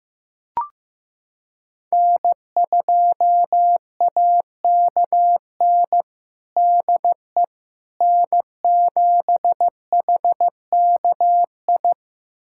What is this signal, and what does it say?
Morse code, heard as a keyed tone, about 15 wpm, 700 Hz. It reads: N2AKN DE N7HKI
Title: USA Callsign DE Callsign - 5 Characters - 15wpm